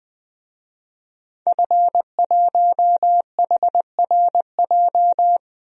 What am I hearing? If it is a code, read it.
Morse code, heard as a keyed tone, about 21 wpm, 700 Hz. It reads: F1HRJ
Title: DXCC-100 Call Signs - 5 Characters - 20wpm